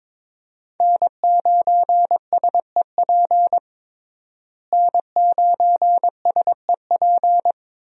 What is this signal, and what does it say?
Morse code, heard as a keyed tone, about 22 wpm, 700 Hz. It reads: N9SEP N9SEP
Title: USA Call Signs - 5 Characters - Repeated 2x - 22wpm